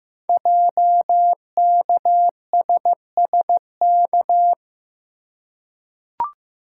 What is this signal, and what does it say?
Morse code, heard as a keyed tone, about 15 wpm, 700 Hz. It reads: JKSSK
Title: Code Groups - 5 Letter:Numbers - 15wpm